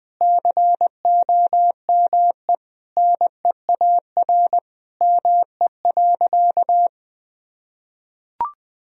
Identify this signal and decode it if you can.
Morse code, heard as a keyed tone, about 21 wpm, 700 Hz. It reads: COME NEAR ME.
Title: Sentences from Top 1000 Words - Review Part 1 - 20wpm